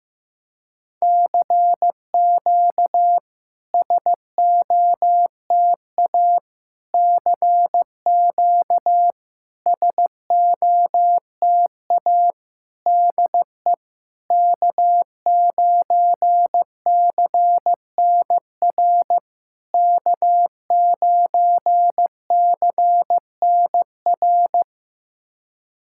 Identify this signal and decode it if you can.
Morse code, heard as a keyed tone, about 15 wpm, 700 Hz. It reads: CQ SOTA CQ SOTA DE K9CNR K9CNR
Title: SOTA QSO Practice 15wpm